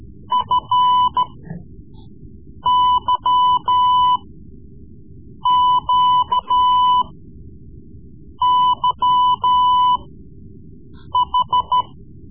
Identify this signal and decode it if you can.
Morse code, heard as a keyed tone, about 14 wpm, spaced as for 5 wpm, 1000 Hz. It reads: FYQYH
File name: Morse Lessons / Morse Lessons / Tape-3B